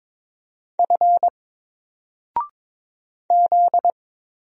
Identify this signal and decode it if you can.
Morse code, heard as a keyed tone, about 22 wpm, 700 Hz. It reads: F Z
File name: Single-Letter-Number-22wpm